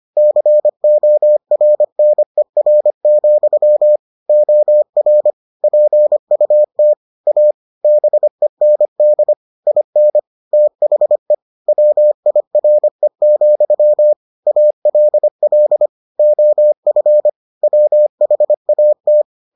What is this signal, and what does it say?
Morse code, heard as a keyed tone, about 25 wpm, 600 Hz. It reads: CORNER, OR PUT A BEND IN THE WIRE, ALL OF WHAT